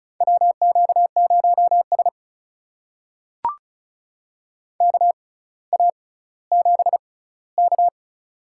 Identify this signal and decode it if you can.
Morse code, heard as a keyed tone, about 36 wpm, spaced as for 18 wpm, 700 Hz. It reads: WQ0S KA7K
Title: USA Call Signs - 4 Characters - 6x Character Spacing - 35wpm